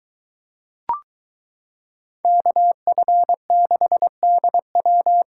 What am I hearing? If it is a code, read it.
Morse code, heard as a keyed tone, about 22 wpm, 700 Hz. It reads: KF6DW